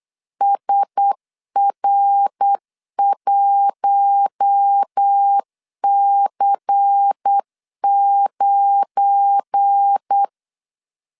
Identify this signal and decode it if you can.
Morse code, heard as a keyed tone, about 8 wpm, 800 Hz. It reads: SR1C9
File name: LEZIONI / 9 2 8wpm 3437